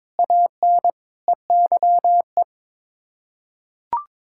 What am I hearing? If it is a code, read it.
Morse code, heard as a keyed tone, about 22 wpm, 700 Hz. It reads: AN EYE
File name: practice / 22wpm / Sets of 6 Words - 22wpm